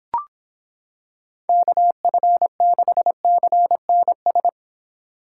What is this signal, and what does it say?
Morse code, heard as a keyed tone, about 26 wpm, 700 Hz. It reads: KF6CNS